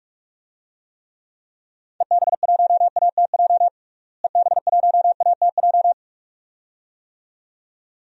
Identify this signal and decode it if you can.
Morse code, heard as a keyed tone, about 43 wpm, 700 Hz. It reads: EB1ATJ EB1ATJ
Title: DXCC-100 Call Signs - 6 Characters - Repeated 2x - 45wpm